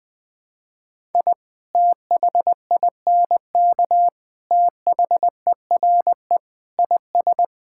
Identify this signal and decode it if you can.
Morse code, heard as a keyed tone, about 20 wpm, 700 Hz. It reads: I THINK THERE IS
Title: Sentences with 6-Word Phrases - 20wpm